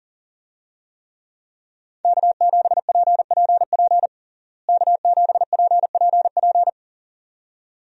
Morse code, heard as a keyed tone, about 40 wpm, 700 Hz. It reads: K7PPP K7PPP
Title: USA Call Signs - 5 Characters - Repeated 2x - 40wpm